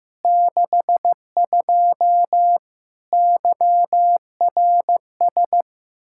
Morse code, heard as a keyed tone, about 15 wpm, 700 Hz. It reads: 62 YRS
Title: Practice QSO - Exchange 3 - More Info - 15wpm